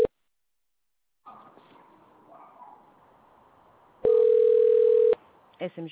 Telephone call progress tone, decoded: ringback tone